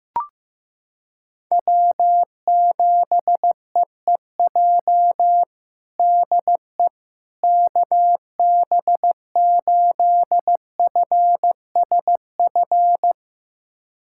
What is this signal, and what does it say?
Morse code, heard as a keyed tone, about 15 wpm, 700 Hz. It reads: W7EEJ DE KB8FSF
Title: USA Callsign DE Callsign 15wpm